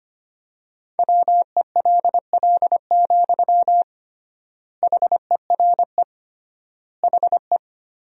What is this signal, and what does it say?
Morse code, heard as a keyed tone, about 25 wpm, 700 Hz. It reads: WELL, HERE HE
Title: 3x Word-Spacing - Sentences from Top 200 Words - 25wpm